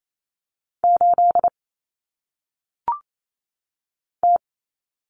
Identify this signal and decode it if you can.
Morse code, heard as a keyed tone, about 28 wpm, 700 Hz. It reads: 8 T
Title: Common Cut Numbers - 1-Digit 28wpm